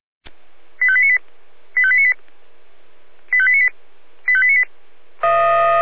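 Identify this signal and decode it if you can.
Five-tone selective call (ZVEI), heard as quick runs of five tones, tones of about 70 ms. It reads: 85898 85898 85898 85898
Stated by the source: ZVEI five-tone selective call received via SDR.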